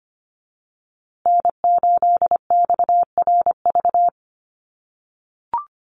Morse code, heard as a keyed tone, about 25 wpm, 700 Hz. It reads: N8XRV
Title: USA Call Signs Repeated 3x 25wpm